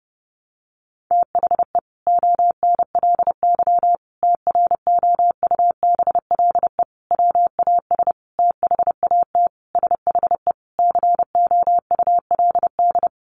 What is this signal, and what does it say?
Morse code, heard as a keyed tone, about 30 wpm, 700 Hz. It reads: THE ONLY TROUBLE WAS THAT SHE COULD
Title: Sentences from Top 1200 Words 30wpm